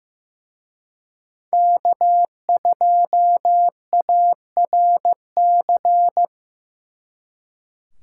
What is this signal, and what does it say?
Morse code, heard as a keyed tone, about 18 wpm, 700 Hz. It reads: K2ARC
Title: Speed Racing - USA Call Signs 15wpm